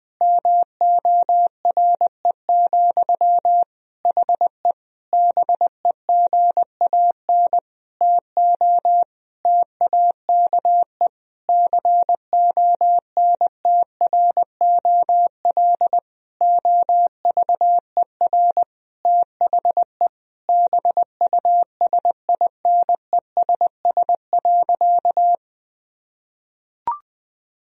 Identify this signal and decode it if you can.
Morse code, heard as a keyed tone, about 20 wpm, 700 Hz. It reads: MORE, HE BEGAN TO TAKE CONTROL OVER THE BUSINESS.
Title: Sentences from Top 1000 Words - Review Part 3 - 20wpm